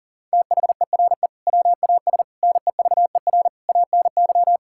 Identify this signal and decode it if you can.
Morse code, heard as a keyed tone, about 40 wpm, 700 Hz. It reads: THERE WAS NEVER ANY
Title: Sentences from Top 1000 Words - Review Part 6 - 40wpm